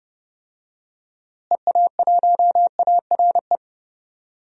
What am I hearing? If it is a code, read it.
Morse code, heard as a keyed tone, about 32 wpm, 700 Hz. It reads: EA1ARE
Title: DXCC-100 Call Signs - 6 Characters - Repeated 2x - 30wpm